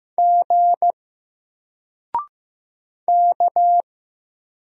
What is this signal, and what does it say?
Morse code, heard as a keyed tone, about 15 wpm, 700 Hz. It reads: G K
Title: Single Letter-Number Alt 15wpm